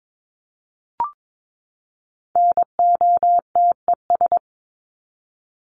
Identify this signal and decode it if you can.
Morse code, heard as a keyed tone, about 22 wpm, 700 Hz. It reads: NOTES
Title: ES:S Suffix - Top 100 Words with Base Words 22wpm